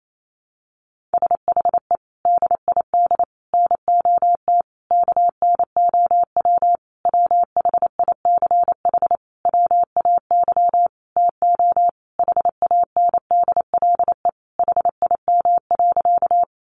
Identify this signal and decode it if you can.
Morse code, heard as a keyed tone, about 28 wpm, 700 Hz. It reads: SHE DID NOT KNOW WHICH WAY TO HANDLE HIM.
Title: Sentences from Top 1100 Words 28wpm